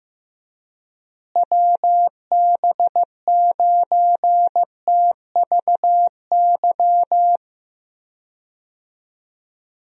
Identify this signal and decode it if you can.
Morse code, heard as a keyed tone, about 15 wpm, 700 Hz. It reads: WB9TVY
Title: USA Call Signs - 6 Characters - Repeated 2x - 15wpm